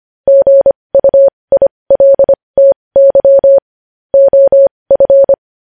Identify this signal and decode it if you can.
Morse code, heard as a keyed tone, about 25 wpm, spaced as for 20 wpm, 550 Hz. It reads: GUILTY OF